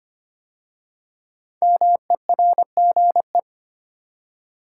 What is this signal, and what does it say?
Morse code, heard as a keyed tone, about 25 wpm, 700 Hz. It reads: MERGE
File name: 5 Letter Words - First 3 Letters Given - 25wpm